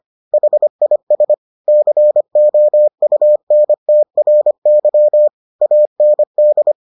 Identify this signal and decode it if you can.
Morse code, heard as a keyed tone, about 25 wpm, 600 Hz. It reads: HIS COUNTRY AND